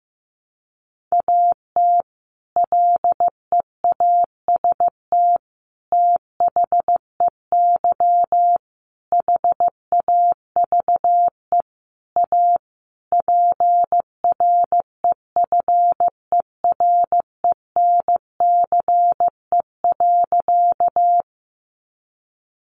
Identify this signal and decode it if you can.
Morse code, heard as a keyed tone, about 15 wpm, 700 Hz. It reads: AT LEAST THEY HAVE A PREFERENCE.
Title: Sentences from Top 1500 Words 15wpm